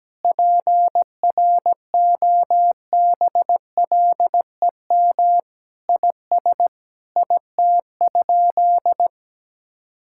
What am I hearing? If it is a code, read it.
Morse code, heard as a keyed tone, about 17 wpm, 700 Hz. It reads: PROBLEM IS IT?
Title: Sentences from Top 1000 Words - Review Part 6 - 17wpm